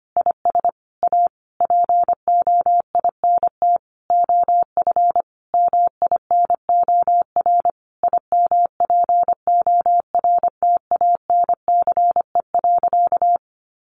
Morse code, heard as a keyed tone, about 25 wpm, 700 Hz. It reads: IS A POINT OF MINOR IMPORTANCE.
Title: Sentences from Top 1500 Words 25wpm